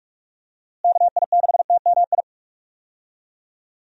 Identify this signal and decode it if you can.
Morse code, heard as a keyed tone, about 43 wpm, 700 Hz. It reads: KI6TMI
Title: USA Call Signs - 6 Characters - Repeated 3x - 45wpm